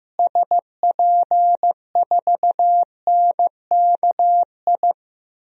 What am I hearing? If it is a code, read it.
Morse code, heard as a keyed tone, about 15 wpm, 700 Hz. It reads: SP4NKI
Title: DXCC-100 Call Signs - 6 Characters - Repeated 2x - 15wpm